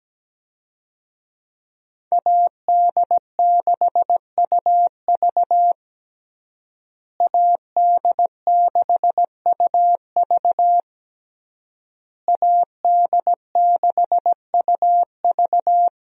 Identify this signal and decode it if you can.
Morse code, heard as a keyed tone, about 17 wpm, 700 Hz. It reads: AD6UV AD6UV AD6UV
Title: USA Call Signs - 5 Characters - Repeated 3x - 17wpm